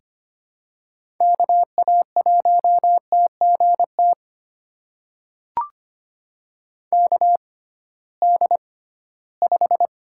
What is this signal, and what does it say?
Morse code, heard as a keyed tone, about 25 wpm, spaced as for 16 wpm, 700 Hz. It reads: KA1TGT KD5
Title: USA Call Signs - 6 Characters - 6x Character Spacing - 25wpm